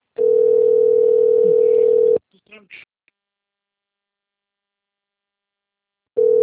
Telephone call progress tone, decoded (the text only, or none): ringback tone